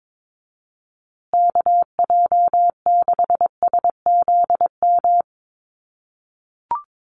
Morse code, heard as a keyed tone, about 22 wpm, 700 Hz. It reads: KJ6SZM
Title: USA Call Signs Repeated 2x 22wpm